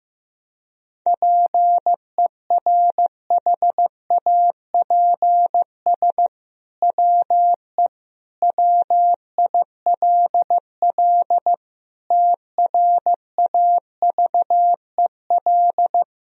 Morse code, heard as a keyed tone, about 15 wpm, 700 Hz. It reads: PERHAPS WE WILL TRAVEL
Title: Sentences from Top 1000 Words - Review Part 2 - 15wpm